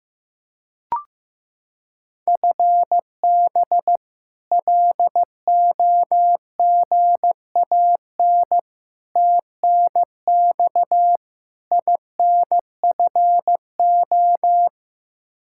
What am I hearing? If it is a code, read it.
Morse code, heard as a keyed tone, about 15 wpm, 700 Hz. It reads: FB LOGAN TNX INFO
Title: K1USN SST QSO Practice 15wpm